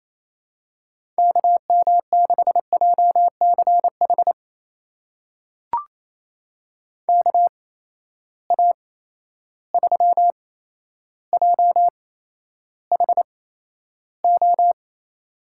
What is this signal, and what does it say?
Morse code, heard as a keyed tone, about 28 wpm, spaced as for 12 wpm, 700 Hz. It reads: KM6JCH KA3JHO
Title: USA Call Signs - 6 Characters - 8x Character Spacing - 28wpm